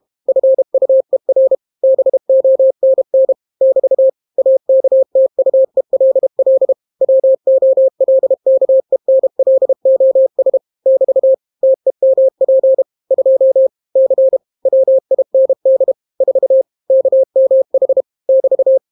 Morse code, heard as a keyed tone, about 32 wpm, 530 Hz. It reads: FUER BONN <BT> AKTUELL WOLKENLOS <BT> TEMP 2 C WIND 4 KMH <BT>